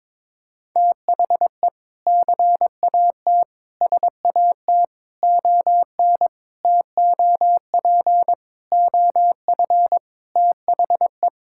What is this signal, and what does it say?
Morse code, heard as a keyed tone, about 22 wpm, 700 Hz. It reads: THE CAT SAT ON TOP OF THE
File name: practice / 22wpm / Sentences with 4-Word Phrases - 22wpm